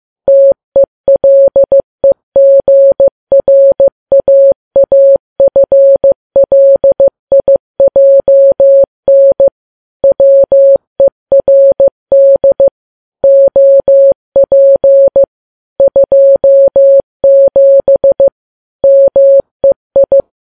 Morse code, heard as a keyed tone, about 15 wpm, 550 Hz. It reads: TELEGRAAFLIJN WERD OP 27 MEI
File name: Files Web tls / Morse on9fv / morse15kps